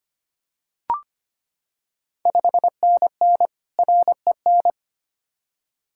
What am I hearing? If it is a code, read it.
Morse code, heard as a keyed tone, about 25 wpm, 700 Hz. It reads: 5NN REN
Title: Ontario QSO Party Exchange 25wpm